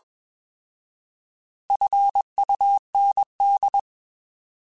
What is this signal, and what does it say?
Morse code, heard as a keyed tone, about 21 wpm, 800 Hz. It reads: FUND
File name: Alphabetical / Alphabetical mp3s / Fs @20